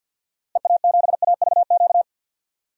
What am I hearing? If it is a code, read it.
Morse code, heard as a keyed tone, about 46 wpm, 700 Hz. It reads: EA7AVQ
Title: DXCC-100 Call Signs - 6 Characters - Repeated 3x - 50wpm